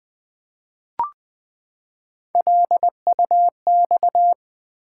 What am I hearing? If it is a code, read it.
Morse code, heard as a keyed tone, about 20 wpm, 700 Hz. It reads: LUX
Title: Build A Word - 3 Letters At A Time - 6-Letter Words - 20wpm